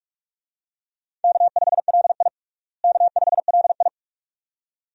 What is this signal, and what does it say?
Morse code, heard as a keyed tone, about 48 wpm, 700 Hz. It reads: K5LI K5LI
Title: USA Call Signs - 4 Characters - Repeated 2x - 45wpm